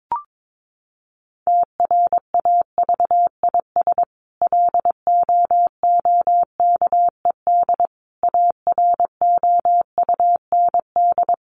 Morse code, heard as a keyed tone, about 22 wpm, 700 Hz. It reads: TRAVIS LOOKED AROUND
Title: A Sound Of Thunder 22wpm